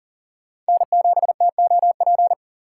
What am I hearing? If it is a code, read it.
Morse code, heard as a keyed tone, about 37 wpm, 700 Hz. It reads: N7TOP
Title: USA Call Signs - 5 Characters - 6x Character Spacing - 40wpm